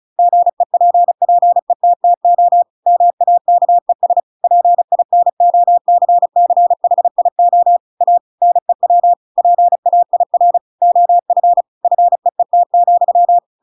Morse code, heard as a keyed tone, about 34 wpm, 700 Hz. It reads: GEPPETTO MAKES PINOCCHIO A NEW PAIR OF FEET,
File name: Pinocchio / Pinocchio ch8 35 35 ik1hge